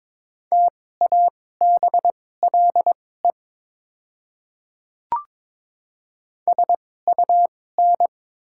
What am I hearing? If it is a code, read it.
Morse code, heard as a keyed tone, about 22 wpm, spaced as for 16 wpm, 700 Hz. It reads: TABLE SUN
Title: 2x Character Spacing - Top 500 Words - 22wpm